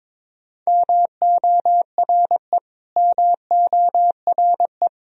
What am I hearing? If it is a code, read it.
Morse code, heard as a keyed tone, about 22 wpm, 700 Hz. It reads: MORE MORE
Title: Top 100 Words Repeated 22wpm